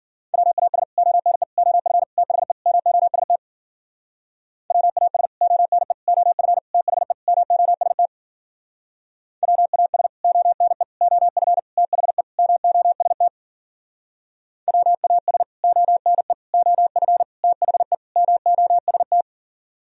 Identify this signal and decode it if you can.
Morse code, heard as a keyed tone, about 49 wpm, 700 Hz. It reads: WAS ONE OF THE MOST WAS ONE OF THE MOST WAS ONE OF THE MOST WAS ONE OF THE MOST
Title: Speed-Racing - Sets of 5 Words 40wpm